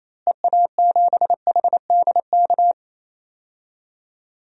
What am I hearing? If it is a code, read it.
Morse code, heard as a keyed tone, about 28 wpm, 700 Hz. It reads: EA7HDK
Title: DXCC-100 Call Signs - 6 Characters - Repeated 3x - 28wpm